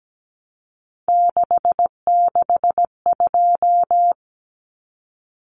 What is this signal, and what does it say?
Morse code, heard as a keyed tone, about 17 wpm, 700 Hz. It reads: 662
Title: Common Cut Numbers - 3-Digit 17wpm